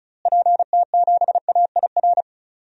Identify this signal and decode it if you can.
Morse code, heard as a keyed tone, about 35 wpm, 700 Hz. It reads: PT7AIR
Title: DXCC-100 Call Signs - 6 Characters - Repeated 2x - 35wpm